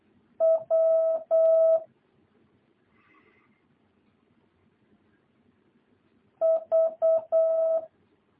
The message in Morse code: W V

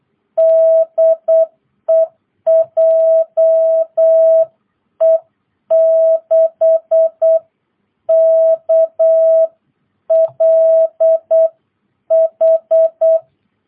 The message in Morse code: DE JE6KLH

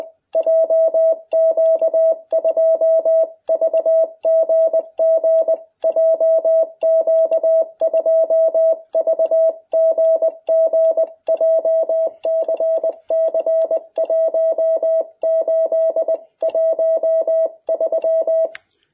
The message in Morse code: JQ2VGG JQ2VGG JCC 1813

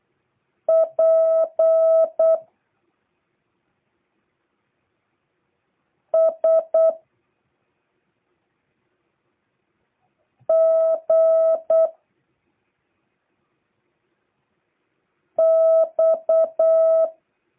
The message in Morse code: P S G X

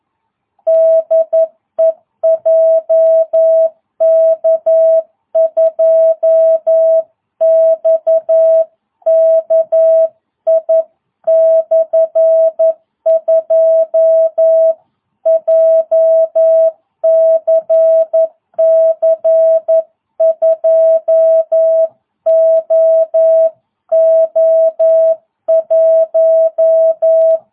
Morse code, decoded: DE JK2XKI/2 JCC 2OO1

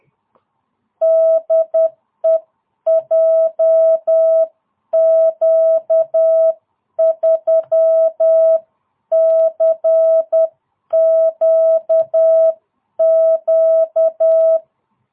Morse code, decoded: DE JQ3CQQ

